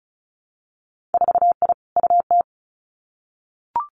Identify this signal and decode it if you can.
Morse code, heard as a keyed tone, about 36 wpm, 700 Hz. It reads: 4I UT
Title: Winter Field Day Exchanges Encore 35wpm